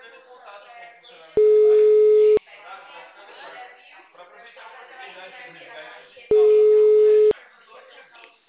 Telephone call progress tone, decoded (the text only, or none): ringback tone